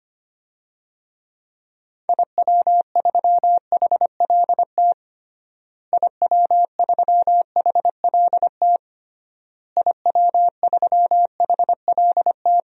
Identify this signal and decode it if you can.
Morse code, heard as a keyed tone, about 25 wpm, 700 Hz. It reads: IW3HLT IW3HLT IW3HLT
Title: DXCC-100 Call Signs - 6 Characters - Repeated 3x - 25wpm